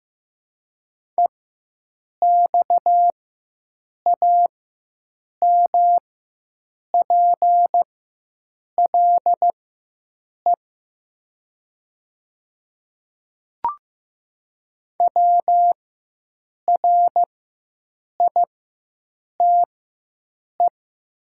Morse code, heard as a keyed tone, about 15 wpm, spaced as for 7 wpm, 700 Hz. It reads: EXAMPLE WRITE
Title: 4x Character Spacing - Top 300 Words - 15wpm